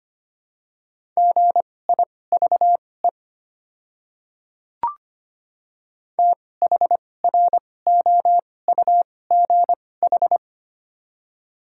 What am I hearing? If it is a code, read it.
Morse code, heard as a keyed tone, about 25 wpm, spaced as for 18 wpm, 690 Hz. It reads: GIVE THROUGH
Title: 2x Character Spacing - Top 200 Words - 25wpm